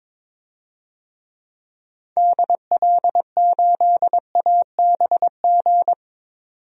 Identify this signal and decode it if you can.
Morse code, heard as a keyed tone, about 22 wpm, 700 Hz. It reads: DL8ABG
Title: DXCC-100 Call Signs - 6 Characters - 22wpm